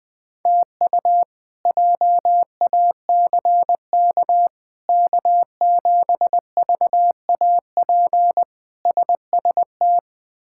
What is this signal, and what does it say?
Morse code, heard as a keyed tone, about 21 wpm, 700 Hz. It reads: TU JACK K7VAP SST
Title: K1USN SST QSO Practice - 20wpm